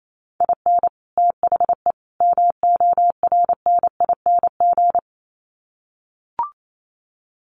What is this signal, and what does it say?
Morse code, heard as a keyed tone, about 28 wpm, 700 Hz. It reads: IN THE MORNING
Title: Sets of 3 Words 28wpm